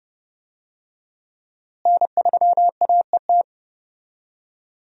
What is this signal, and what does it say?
Morse code, heard as a keyed tone, about 31 wpm, 700 Hz. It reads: N3AET
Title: USA Call Signs - 5 Characters - Repeated 2x - 30wpm